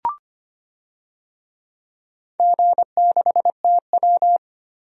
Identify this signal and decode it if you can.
Morse code, heard as a keyed tone, about 25 wpm, 700 Hz. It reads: G6TW
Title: DXCC-100 Call Signs - 4 Characters - 25wpm